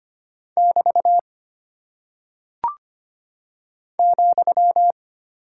Morse code, heard as a keyed tone, about 25 wpm, 700 Hz. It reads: <BT> ,
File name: Single Punctuation-Prosign - 25wpm